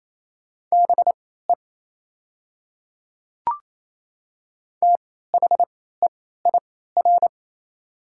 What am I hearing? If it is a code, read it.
Morse code, heard as a keyed tone, about 27 wpm, spaced as for 16 wpm, 690 Hz. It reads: BE THEIR
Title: 3x Character Spacing - Top 100 Words - 28wpm